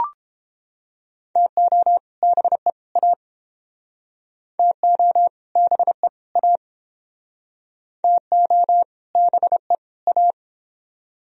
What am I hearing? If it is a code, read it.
Morse code, heard as a keyed tone, about 29 wpm, 700 Hz. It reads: TO BE A TO BE A TO BE A